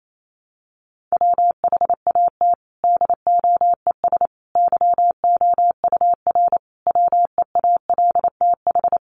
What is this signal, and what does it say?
Morse code, heard as a keyed tone, about 28 wpm, 700 Hz. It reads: WHAT DOES YOUR WEALTH
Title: Sentences from Top 1400 Words 28wpm